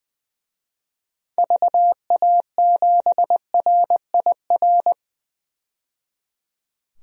Morse code, heard as a keyed tone, about 20 wpm, 700 Hz. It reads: VA7RIR
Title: DXCC-100 Call Signs - 6 Characters - 20wpm